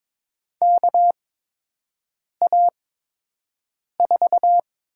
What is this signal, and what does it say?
Morse code, heard as a keyed tone, about 23 wpm, spaced as for 6 wpm, 700 Hz. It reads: KA4
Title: USA Call Signs - 6 Characters - 8x Character Spacing - 22wpm